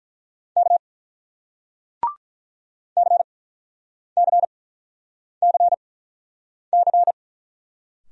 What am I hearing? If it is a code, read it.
Morse code, heard as a keyed tone, about 46 wpm, spaced as for 9 wpm, 700 Hz. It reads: K CCCC